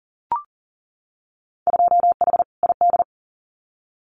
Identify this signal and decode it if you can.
Morse code, heard as a keyed tone, about 40 wpm, 700 Hz. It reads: 2H ID
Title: Winter Field Day Exchanges 40wpm